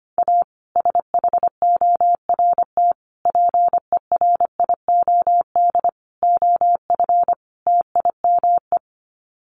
Morse code, heard as a keyed tone, about 25 wpm, 700 Hz. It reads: A SHORT PERIOD OF TIME
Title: Sets of 5 Words Encore 25wpm